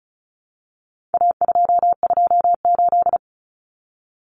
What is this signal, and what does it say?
Morse code, heard as a keyed tone, about 34 wpm, 700 Hz. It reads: A228
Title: Common Cut Numbers - 4-Digit 35wpm